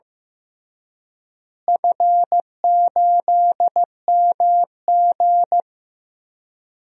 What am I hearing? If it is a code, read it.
Morse code, heard as a keyed tone, about 15 wpm, 700 Hz. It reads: F8MG
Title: DXCC-100 Call Signs - 4 Characters - Repeated 3x - 15wpm